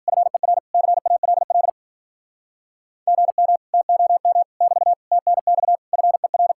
Morse code, heard as a keyed tone, about 52 wpm, 700 Hz. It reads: FER CALL GM TOM <BT> TNX FER